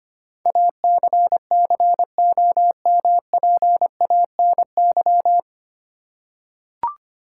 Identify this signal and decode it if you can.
Morse code, heard as a keyed tone, about 25 wpm, 700 Hz. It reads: ACCOMPANY
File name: Nine Letter Words - 25wpm